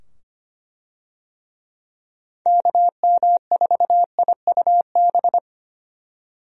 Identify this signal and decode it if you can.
Morse code, heard as a keyed tone, about 25 wpm, 700 Hz. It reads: KM4IUB
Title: USA Call Signs - 6 Characters - 25wpm